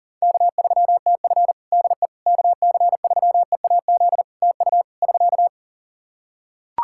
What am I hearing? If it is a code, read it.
Morse code, heard as a keyed tone, about 40 wpm, 700 Hz. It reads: K3TF DE KC3EAZ TU <SK>
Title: Practice QSO - Exchange 4 - Ending - 40wpm